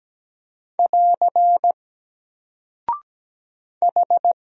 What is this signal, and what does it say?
Morse code, heard as a keyed tone, about 17 wpm, 700 Hz. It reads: <AR> H
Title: Single Character (Letter:Number:Punctuation:Prosign) 17wpm